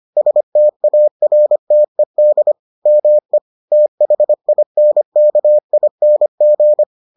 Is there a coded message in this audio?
Yes: Morse code from 0.2 to 6.8 s, heard as a keyed tone, about 25 wpm, 600 Hz. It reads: STARTED ME THINKING